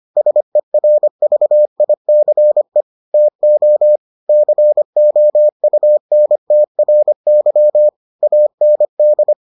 Morse code, heard as a keyed tone, about 25 wpm, 600 Hz. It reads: SERVICE TO COUNTRY AND